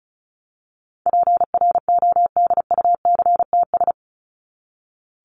Morse code, heard as a keyed tone, about 34 wpm, 700 Hz. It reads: PRODUCTS
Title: ES:S Suffix - Top 100 Words with Base Words 35wpm